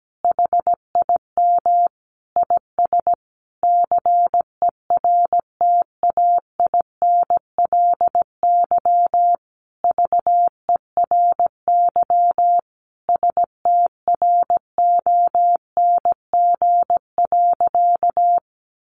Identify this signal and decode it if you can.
Morse code, heard as a keyed tone, about 17 wpm, 700 Hz. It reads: HIM IS CERTAINLY VERY STRONG.
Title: Sentences from Top 1300 Words 17wpm